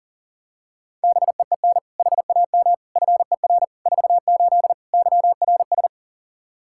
Morse code, heard as a keyed tone, about 40 wpm, 700 Hz. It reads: BEEN HAM FER 48 YRS
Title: Practice QSO - Exchange 3 - More Info - 40wpm